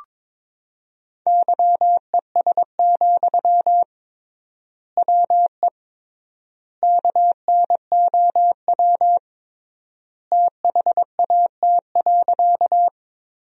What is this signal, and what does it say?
Morse code, heard as a keyed tone, about 22 wpm, 700 Hz. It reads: YES, WE KNOW THAT.
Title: 3x Word-Spacing - Sentences from Top 500 Words - 22wpm